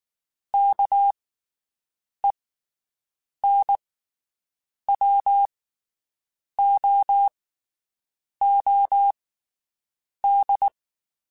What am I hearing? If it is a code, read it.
Morse code, heard as a keyed tone, about 19 wpm, spaced as for 7 wpm, 800 Hz. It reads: KENWOOD
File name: (spaced) / QSOs Spaced-mp3s / WD0BGP 1 @18 sp